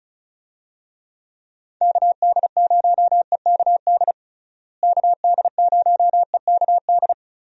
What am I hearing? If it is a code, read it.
Morse code, heard as a keyed tone, about 34 wpm, 700 Hz. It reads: KD0EKD KD0EKD
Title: USA Call Signs - 6 Characters - Repeated 3x - 35wpm